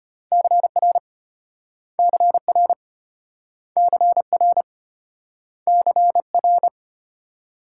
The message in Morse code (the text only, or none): CR CR CR CR